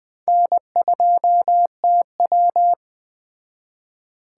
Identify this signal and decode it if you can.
Morse code, heard as a keyed tone, about 20 wpm, 700 Hz. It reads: N2TW
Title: Speed Racing - USA 1x2 Call Signs 17wpm